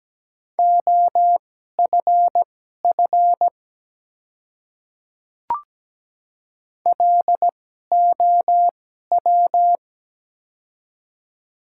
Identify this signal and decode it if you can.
Morse code, heard as a keyed tone, about 17 wpm, spaced as for 12 wpm, 700 Hz. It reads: OFF LOW
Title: 2x Character Spacing - Top 200 Words - 17wpm